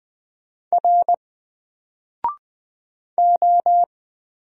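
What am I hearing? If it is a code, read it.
Morse code, heard as a keyed tone, about 20 wpm, 700 Hz. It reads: R O
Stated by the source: Single letter and number Morse drill at 20 wpm, from N4LCD.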